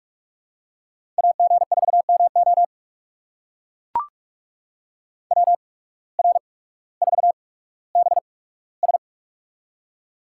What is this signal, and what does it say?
Morse code, heard as a keyed tone, about 44 wpm, spaced as for 19 wpm, 700 Hz. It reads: AG4MO WR4BS
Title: USA Call Signs - 5 Characters - 8x Character Spacing - 45wpm